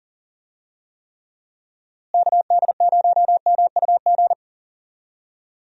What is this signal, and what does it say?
Morse code, heard as a keyed tone, about 39 wpm, 700 Hz. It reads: KD0MUG
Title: USA Call Signs - 6 Characters - Repeated 3x - 40wpm